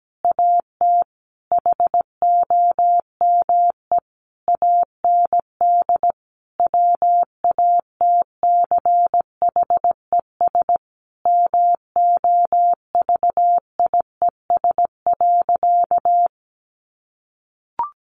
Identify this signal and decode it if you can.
Morse code, heard as a keyed tone, about 17 wpm, 700 Hz. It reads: AT HOME AND WATCHES MOVIES.
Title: Additional Easy Phrases 17wpm